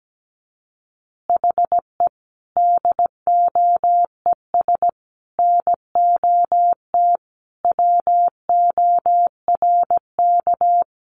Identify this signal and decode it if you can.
Morse code, heard as a keyed tone, about 17 wpm, 700 Hz. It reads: HE DOES NOT WORK
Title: Additional Easy Phrases 17wpm